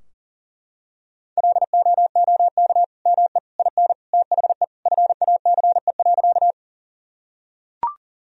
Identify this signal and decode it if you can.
Morse code, heard as a keyed tone, about 40 wpm, 700 Hz. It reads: LOOK ME IN THE FACE.
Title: Sentences from Top 1000 Words - Review Part 2 - 40wpm